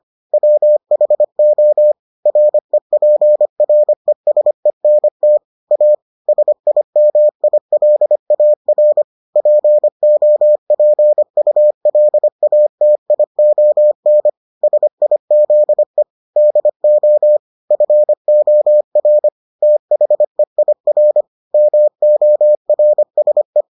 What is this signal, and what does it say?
Morse code, heard as a keyed tone, about 25 wpm, 600 Hz. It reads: WHO REPRESENT A SIMILAR POPULATION SIZE DO FOR THEIR MORSE